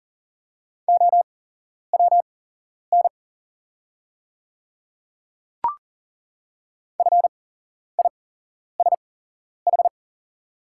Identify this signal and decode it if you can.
Morse code, heard as a keyed tone, about 40 wpm, spaced as for 11 wpm, 700 Hz. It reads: OWN FISH
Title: 8x Character Spacing - Top 300 Words - 40wpm